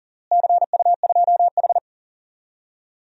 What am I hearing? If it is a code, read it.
Morse code, heard as a keyed tone, about 40 wpm, 700 Hz. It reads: CU2H
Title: DXCC-100 Call Signs - 4 Characters - Repeated 3x - 40wpm